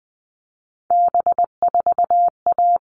Morse code, heard as a keyed tone, about 20 wpm, 700 Hz. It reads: B4A
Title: All Cut Numbers - 3-Digit 20wpm